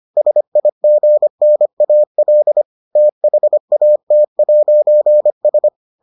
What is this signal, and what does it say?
Morse code, heard as a keyed tone, about 25 wpm, 600 Hz. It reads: SIGNAL THAT'S